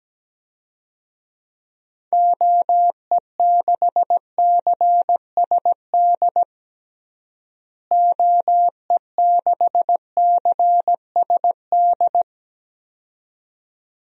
Morse code, heard as a keyed tone, about 17 wpm, 700 Hz. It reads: OE6CSD OE6CSD
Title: DXCC-100 Call Signs - 6 Characters - Repeated 2x - 17wpm